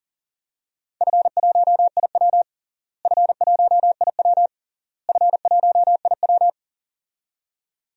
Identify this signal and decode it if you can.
Morse code, heard as a keyed tone, about 40 wpm, 700 Hz. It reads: F1IW F1IW F1IW
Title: DXCC-100 Call Signs - 4 Characters - Repeated 3x - 40wpm